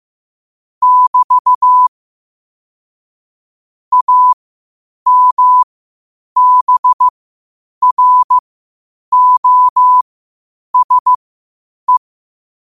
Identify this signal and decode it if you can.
Morse code, heard as a keyed tone, about 15 wpm, spaced as for 8 wpm, 1000 Hz. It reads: <BT> AMBROSE
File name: qotdfiles / 8wpm